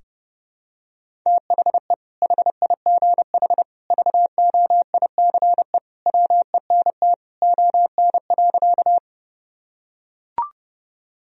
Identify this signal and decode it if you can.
Morse code, heard as a keyed tone, about 30 wpm, 700 Hz. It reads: THE HIGH VOICE WENT ON.
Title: Sentences from Top 1000 Words - Review Part 1 - 30wpm